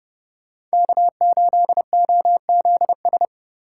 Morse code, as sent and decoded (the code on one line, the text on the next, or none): -.- ---.. --- --.. ...
K8OZS